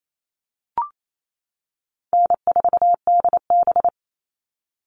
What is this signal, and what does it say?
Morse code, heard as a keyed tone, about 28 wpm, 700 Hz. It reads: N4DB
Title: All Cut Numbers - 4-Digit 28wpm